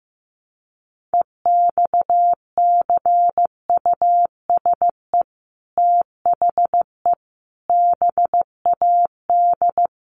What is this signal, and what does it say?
Morse code, heard as a keyed tone, about 15 wpm, 700 Hz. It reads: EXCUSE THE BAD
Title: Sentences from Top 800 Words 15wpm